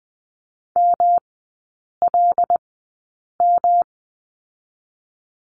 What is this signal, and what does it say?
Morse code, heard as a keyed tone, about 20 wpm, 700 Hz. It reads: M L M
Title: 107 Lesson 18e - Letter m - Three Characters 20wpm